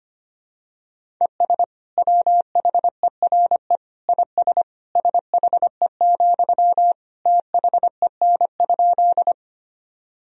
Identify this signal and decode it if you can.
Morse code, heard as a keyed tone, about 25 wpm, 700 Hz. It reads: ES WHERE IS SHE, THEN?
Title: Sentences from Top 200 Words and Abbreviations 25wpm